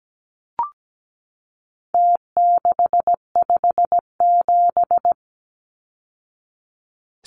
Morse code, heard as a keyed tone, about 17 wpm, 700 Hz. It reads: T657
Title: Numbers - Contest Style - 4-Digit 17wpm